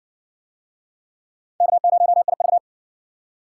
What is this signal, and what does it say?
Morse code, heard as a keyed tone, about 57 wpm, 700 Hz. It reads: K0IV